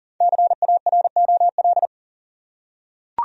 Morse code, heard as a keyed tone, about 40 wpm, 700 Hz. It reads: CAROL
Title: USA Common Names - 40wpm